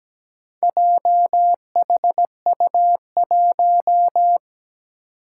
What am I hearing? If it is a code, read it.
Morse code, heard as a keyed tone, about 17 wpm, 700 Hz. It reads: JHU1
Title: Four Challenging Letter-Number Combinations 17wpm